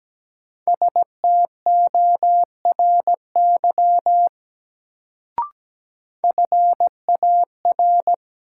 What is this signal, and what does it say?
Morse code, heard as a keyed tone, about 17 wpm, 700 Hz. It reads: STORY FAR
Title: Sets of 3 Words - New Word Drill - 17wpm